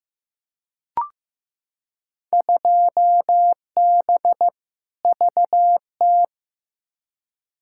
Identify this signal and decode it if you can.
Morse code, heard as a keyed tone, about 15 wpm, 700 Hz. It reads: 2B VT
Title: Field Day Exchanges Encore - 15wpm